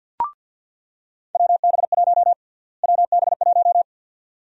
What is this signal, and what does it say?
Morse code, heard as a keyed tone, about 48 wpm, 700 Hz. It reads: WB1 WB1
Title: USA Call Sign Prefixes - Repeated 2x - 50wpm